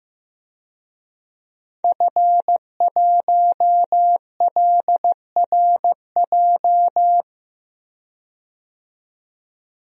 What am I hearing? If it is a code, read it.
Morse code, heard as a keyed tone, about 15 wpm, 700 Hz. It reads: F1LRJ
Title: DXCC-100 Call Signs - 5 Characters - 15wpm